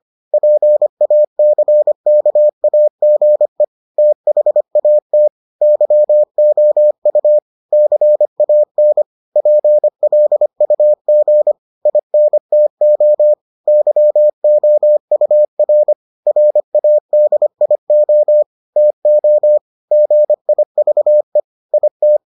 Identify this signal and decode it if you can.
Morse code, heard as a keyed tone, about 25 wpm, 600 Hz. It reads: PACKAGE THAT YOU CAN PLUG INTO YOUR RADIO TO GIVE IT